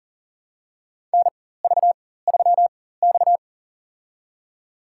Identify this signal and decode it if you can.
Morse code, heard as a keyed tone, about 40 wpm, spaced as for 19 wpm, 700 Hz. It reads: NV3X
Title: USA Call Signs - 4 Characters - 4x Character Spacing - 40wpm